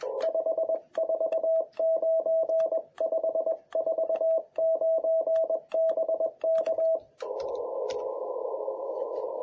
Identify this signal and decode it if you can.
Morse code, heard as a keyed tone, about 22 wpm, 650 Hz. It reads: 549 549 BK
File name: jh8cbh / 2021cw